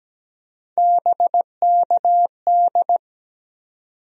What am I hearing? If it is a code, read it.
Morse code, heard as a keyed tone, about 17 wpm, 700 Hz. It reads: BKD
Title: USA Call Sign Postfixes - 3 Letters 17wpm